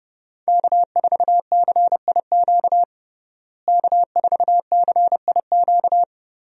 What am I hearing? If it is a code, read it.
Morse code, heard as a keyed tone, about 30 wpm, 700 Hz. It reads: K4CIQ K4CIQ
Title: USA Call Signs - 5 Characters - Repeated 3x - 30wpm